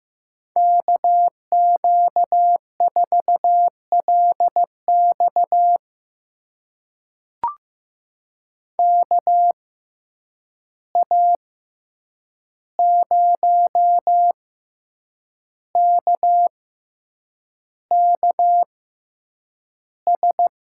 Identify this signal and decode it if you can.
Morse code, heard as a keyed tone, about 16 wpm, spaced as for 7 wpm, 700 Hz. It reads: KQ4LX KA0KKS
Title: USA Call Signs - 6x Character Spacing - 15wpm